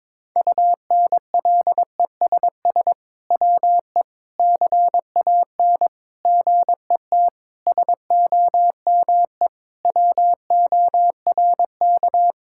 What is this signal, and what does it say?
Morse code, heard as a keyed tone, about 22 wpm, 700 Hz. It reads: UNLESS WE CAN GET SOME WORK
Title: Sentences from Top 1000 Words - Review Part 6 - 22wpm